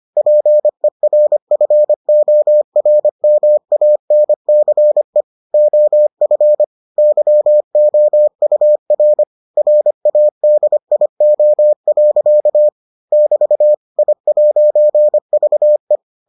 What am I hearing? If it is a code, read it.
Morse code, heard as a keyed tone, about 25 wpm, 600 Hz. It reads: PERFORMANCE OF YOUR RADIO. <BT> I'VE